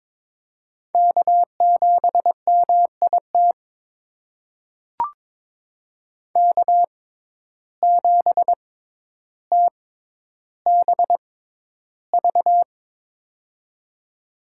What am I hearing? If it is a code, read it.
Morse code, heard as a keyed tone, about 22 wpm, spaced as for 11 wpm, 700 Hz. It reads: K7MIT K7TBV
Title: USA Call Signs - 5 Characters - 6x Character Spacing - 22wpm